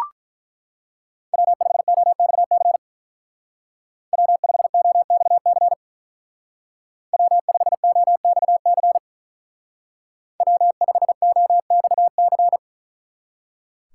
Morse code, heard as a keyed tone, about 44 wpm, 700 Hz. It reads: W5OXC W5OXC W5OXC W5OXC
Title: Speed Racing - USA 1x3 Call Signs 35wpm